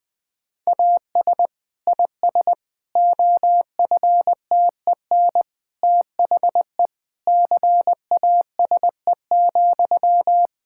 Morse code, heard as a keyed tone, about 20 wpm, 700 Hz. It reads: AS IS OFTEN THE CASE,
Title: Sentences from Top 1000 Words - Review Part 5 - 20wpm